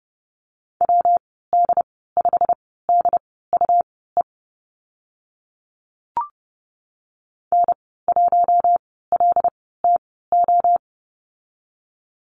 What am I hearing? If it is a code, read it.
Morse code, heard as a keyed tone, about 30 wpm, spaced as for 17 wpm, 700 Hz. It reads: WD5DUE N1LTO
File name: Lesson 19g - Letter w - Call Signs 30wpm